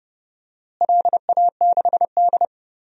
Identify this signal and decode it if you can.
Morse code, heard as a keyed tone, about 30 wpm, 700 Hz. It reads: LA6D
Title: DXCC-100 Call Signs - 4 Characters - Repeated 2x - 30wpm